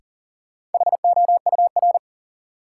Morse code, heard as a keyed tone, about 40 wpm, 700 Hz. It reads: HOUR
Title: Sentences from Top 400 Words - New Word Drill - 40wpm